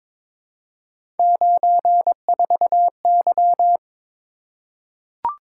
Morse code, heard as a keyed tone, about 22 wpm, 700 Hz. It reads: 94Y
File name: practice / 22wpm / Three Letter-Number Combinations - 22wpm